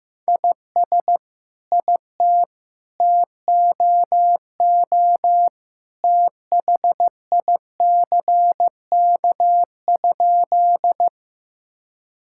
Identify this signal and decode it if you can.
Morse code, heard as a keyed tone, about 15 wpm, 700 Hz. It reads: IS IT TOO THICK?
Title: Sentences from Top 1600 Words 15wpm